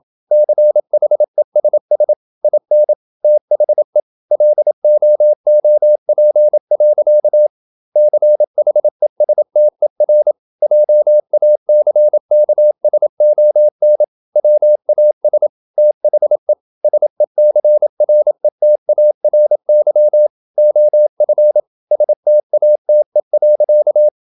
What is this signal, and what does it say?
Morse code, heard as a keyed tone, about 28 wpm, 600 Hz. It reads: CHESS IN THE LOOP. CHESTER JACKSON WAS THE SECRETARY OF STATE.
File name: Never 0027wpm